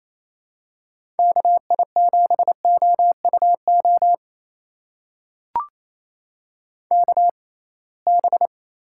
Morse code, heard as a keyed tone, about 28 wpm, spaced as for 21 wpm, 700 Hz. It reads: KI7OUO KB
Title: USA Call Signs - 6 Characters - 6x Character Spacing - 28wpm